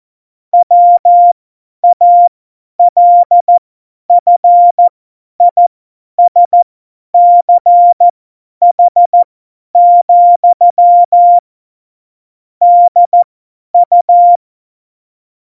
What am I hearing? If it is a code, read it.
Morse code, heard as a keyed tone, about 14 wpm, spaced as for 10 wpm, 700 Hz. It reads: WALFISCH, DU
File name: Morsekurs / Witze / Witz 13. November 2018, 50 F70